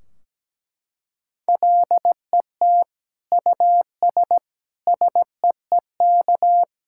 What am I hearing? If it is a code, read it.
Morse code, heard as a keyed tone, about 17 wpm, 700 Hz. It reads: LET US SEEK
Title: Sentences from Top 1600 Words 17wpm